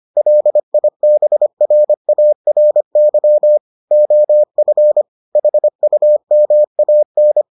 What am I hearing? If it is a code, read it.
Morse code, heard as a keyed tone, about 25 wpm, 600 Hz. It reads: LIBRARY OF HUMAN